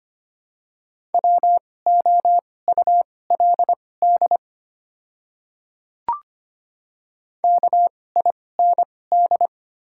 Morse code, heard as a keyed tone, about 25 wpm, spaced as for 18 wpm, 700 Hz. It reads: WOULD KIND
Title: 2x Character Spacing - Top 300 Words - 25wpm